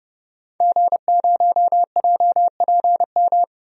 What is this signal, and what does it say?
Morse code, heard as a keyed tone, about 30 wpm, 700 Hz. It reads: G0JPM
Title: DXCC-100 Call Signs - 5 Characters - 30wpm